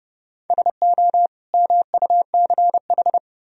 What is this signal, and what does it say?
Morse code, heard as a keyed tone, about 37 wpm, 700 Hz. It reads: SO MUCH
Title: Speed-Racing - Sets of 2 Words 30wpm